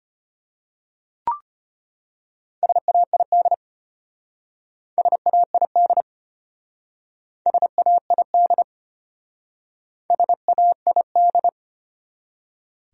Morse code, heard as a keyed tone, about 33 wpm, 700 Hz. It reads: SAID SAID SAID SAID